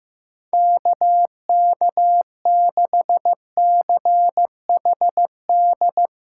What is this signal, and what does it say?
Morse code, heard as a keyed tone, about 15 wpm, 700 Hz. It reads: KK6CHD
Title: USA Callsign DE Callsign - 6 Characters - 15wpm